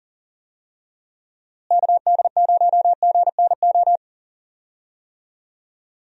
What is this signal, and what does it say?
Morse code, heard as a keyed tone, about 40 wpm, 700 Hz. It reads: KD0GNO
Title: USA Call Signs - 6 Characters - Repeated 2x - 40wpm